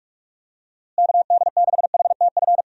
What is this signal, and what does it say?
Morse code, heard as a keyed tone, about 46 wpm, 700 Hz. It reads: KD6HTF
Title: USA Call Signs - 6 Characters - Repeated 2x - 45wpm